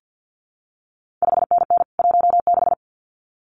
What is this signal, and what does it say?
Morse code, heard as a keyed tone, about 50 wpm, 700 Hz. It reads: 5NN 16
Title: CQ WW DX Contest Exchange - 50wpm